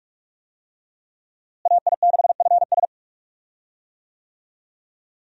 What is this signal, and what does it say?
Morse code, heard as a keyed tone, about 46 wpm, 700 Hz. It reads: AI6FS
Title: USA Call Signs - 5 Characters - Repeated 3x - 45wpm